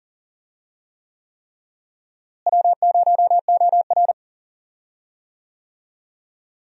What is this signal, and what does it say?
Morse code, heard as a keyed tone, about 40 wpm, 700 Hz. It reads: W0OR
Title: USA Call Signs - 4 Characters - Repeated 3x - 40wpm